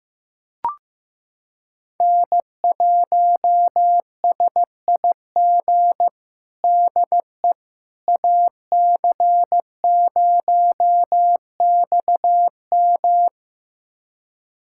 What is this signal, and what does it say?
Morse code, heard as a keyed tone, about 15 wpm, 700 Hz. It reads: N1SIG DE AC0XM
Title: USA Callsign DE Callsign - 5 Characters - 15wpm